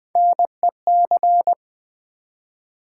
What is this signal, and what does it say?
Morse code, heard as a keyed tone, about 20 wpm, 700 Hz. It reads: NEC
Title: Build A Word - 1 Letter At A Time - 4-Letter Words - 20wpm